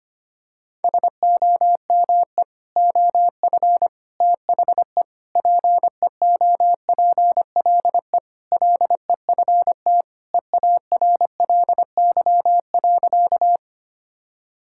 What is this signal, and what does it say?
Morse code, heard as a keyed tone, about 25 wpm, 700 Hz. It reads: SOME OF THE PEOPLE LEFT EARLY.
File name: practice / 25wpm / Sentences with 3-Word Phrases - 25wpm